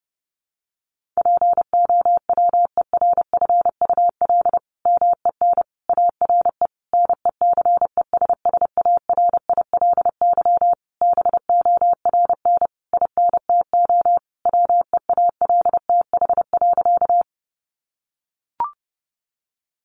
Morse code, heard as a keyed tone, about 30 wpm, 700 Hz. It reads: POWERFUL MEN ARE NECESSARILY BORN INTO WEALTH.
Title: Sentences from Top 1400 Words 30wpm